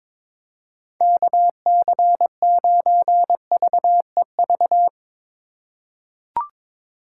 Morse code, heard as a keyed tone, about 21 wpm, 700 Hz. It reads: KC9VEV
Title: USA Call Signs - 6 Characters - 2x Character Spacing - 22wpm